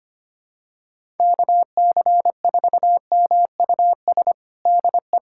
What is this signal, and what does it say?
Morse code, heard as a keyed tone, about 25 wpm, 700 Hz. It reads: KC4MUS DE
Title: USA Callsign DE Callsign 25wpm